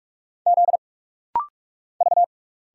Morse code, heard as a keyed tone, about 46 wpm, 700 Hz. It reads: Z V
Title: Single Letters - Rapid-Fire 45wpm